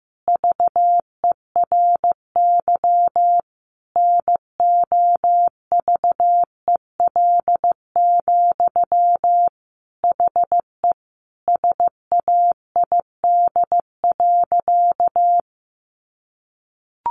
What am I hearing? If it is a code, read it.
Morse code, heard as a keyed tone, about 15 wpm, 700 Hz. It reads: VERY NOVEL, HE SAID.
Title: Sentences from Top 1300 Words 15wpm